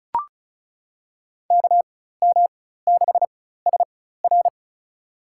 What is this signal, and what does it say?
Morse code, heard as a keyed tone, about 36 wpm, spaced as for 16 wpm, 700 Hz. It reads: KM6SR
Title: USA Call Signs - 5 Characters - 4x Character Spacing - 35wpm